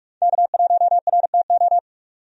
Morse code, heard as a keyed tone, about 43 wpm, 700 Hz. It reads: K1RTO
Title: USA Call Signs - 5 Characters - 45wpm